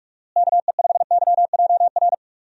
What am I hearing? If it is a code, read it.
Morse code, heard as a keyed tone, about 41 wpm, 700 Hz. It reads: KE5YJR
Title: USA Call Signs - 6 Characters - Repeated 3x - 45wpm